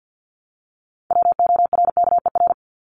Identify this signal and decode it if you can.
Morse code, heard as a keyed tone, about 50 wpm, 700 Hz. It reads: WORKER
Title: Top 1751-2000 Words 50wpm